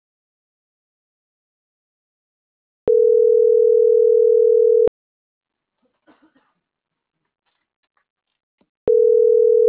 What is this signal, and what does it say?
Ringback tone (North America), 440 + 480 Hz.